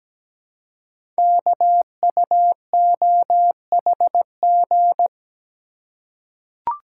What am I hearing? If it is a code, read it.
Morse code, heard as a keyed tone, about 17 wpm, 700 Hz. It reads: KUOHG
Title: Code Groups - 5 Letters - 17wpm